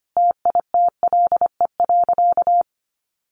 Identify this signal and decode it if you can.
Morse code, heard as a keyed tone, about 25 wpm, 700 Hz. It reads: TITLE.
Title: Sentences from Top 1200 Words 25wpm